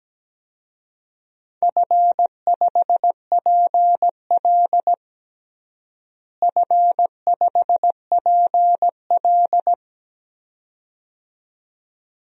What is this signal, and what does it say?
Morse code, heard as a keyed tone, about 17 wpm, 700 Hz. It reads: F5PL F5PL
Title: DXCC-100 Call Signs - 4 Characters - Repeated 2x - 17wpm